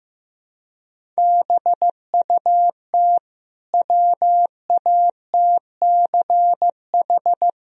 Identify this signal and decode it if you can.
Morse code, heard as a keyed tone, about 15 wpm, 700 Hz. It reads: BUT WATCH
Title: Sentences from Top 1000 Words - Review Part 1 - 15wpm